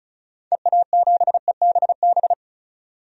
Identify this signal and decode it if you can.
Morse code, heard as a keyed tone, about 34 wpm, 700 Hz. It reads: EA7EBB
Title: DXCC-100 Call Signs - 6 Characters - 35wpm